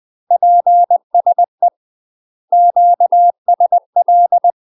Morse code, heard as a keyed tone, about 20 wpm, 700 Hz. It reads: PSE QSL
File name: UK MASTER QSOs TRAINING / 20WPM / UK MASTER QSOs 0018